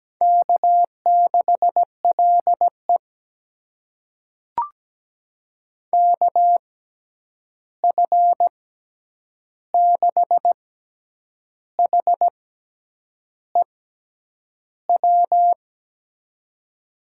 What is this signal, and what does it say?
Morse code, heard as a keyed tone, about 17 wpm, spaced as for 8 wpm, 700 Hz. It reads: K6LE KF6HEW
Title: USA Call Signs - 6x Character Spacing - 17wpm